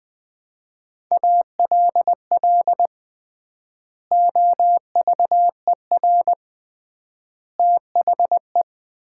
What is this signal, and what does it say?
Morse code, heard as a keyed tone, about 20 wpm, 700 Hz. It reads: ALL OVER THE
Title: 3x Word-Spacing - Sets of 3 Words - 20wpm